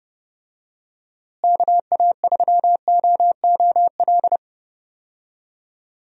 Morse code, heard as a keyed tone, about 30 wpm, 700 Hz. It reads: KA3OOL
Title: USA Call Signs - 6 Characters - Repeated 3x - 30wpm